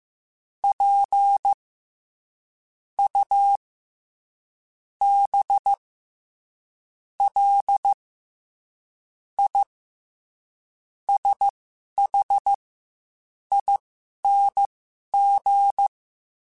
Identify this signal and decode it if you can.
Morse code, heard as a keyed tone, about 15 wpm, spaced as for 6 wpm, 800 Hz. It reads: PUBLISHING
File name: Special INGs-mp3s / Special INGs-mp3s / SlowToFaster ing @10